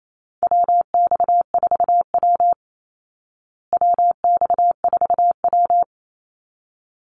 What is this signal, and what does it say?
Morse code, heard as a keyed tone, about 28 wpm, 700 Hz. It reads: WX4W WX4W
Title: Call Signs Repeated 3x - 28wpm